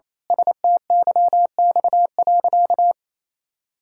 Morse code, heard as a keyed tone, about 28 wpm, 700 Hz. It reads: STYX.
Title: Pangram Challenge v2 - 28wpm